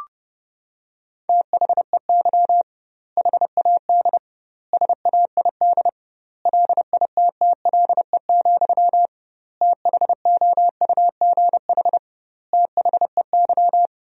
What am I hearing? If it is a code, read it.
Morse code, heard as a keyed tone, about 30 wpm, 700 Hz. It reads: THEY HAD SAID LITTLE, THOUGH THEY
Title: 2x Word-Spacing - Sentences from Top 400 Words - 30wpm